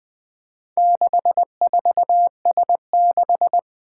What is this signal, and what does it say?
Morse code, heard as a keyed tone, about 20 wpm, 700 Hz. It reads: 64S6
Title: Four Challenging Letter-Number Combinations 20wpm